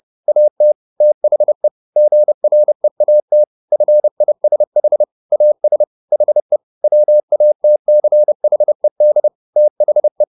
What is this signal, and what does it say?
Morse code, heard as a keyed tone, about 30 wpm, 600 Hz. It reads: AT THE GREAT FISH AS HE WATCHED THE